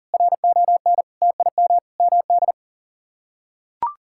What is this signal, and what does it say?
Morse code, heard as a keyed tone, about 40 wpm, 700 Hz. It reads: RON TIM MD
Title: K1USN SST QSO Practice - 40wpm